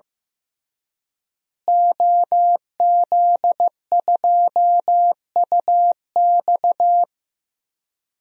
Morse code, heard as a keyed tone, about 15 wpm, 700 Hz. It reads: OZ2UX
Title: DXCC-100 Call Signs - 5 Characters - Repeated 3x - 15wpm